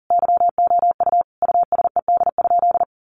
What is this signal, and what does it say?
Morse code, heard as a keyed tone, about 40 wpm, 700 Hz. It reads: YOU USED?
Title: Sentences from Top 1200 Words 40wpm